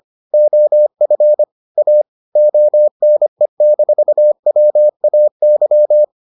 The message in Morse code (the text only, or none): OF A ONE-WAY